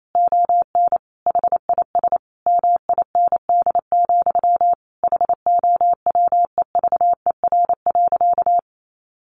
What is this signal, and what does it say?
Morse code, heard as a keyed tone, about 28 wpm, 700 Hz. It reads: ON HIS MIND, HOWEVER.
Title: Sentences from Top 1100 Words 28wpm